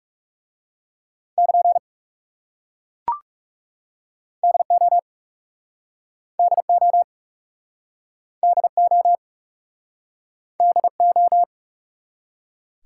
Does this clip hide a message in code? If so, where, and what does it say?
Yes: Morse code from 1.4 to 11.4 s, heard as a keyed tone, about 39 wpm, 700 Hz. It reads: <KN> DO DO DO DO